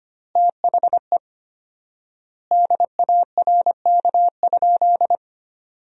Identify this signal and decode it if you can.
Morse code, heard as a keyed tone, about 25 wpm, 700 Hz. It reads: THE DARK?
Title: 4x Word-Spacing - Sentences from Top 500 Words - 25wpm